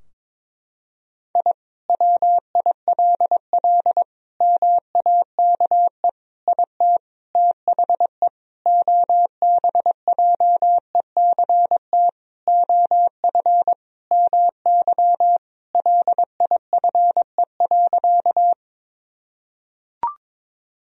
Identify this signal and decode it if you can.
Morse code, heard as a keyed tone, about 22 wpm, 700 Hz. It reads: I WILL MAKE IT THE OBJECT OF MY LIFE.
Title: Sentences from Top 1000 Words - Review Part 3 - 22wpm